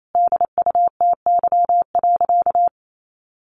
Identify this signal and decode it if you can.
Morse code, heard as a keyed tone, about 28 wpm, 700 Hz. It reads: DUTY.
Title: Sentences from Top 1500 Words 28wpm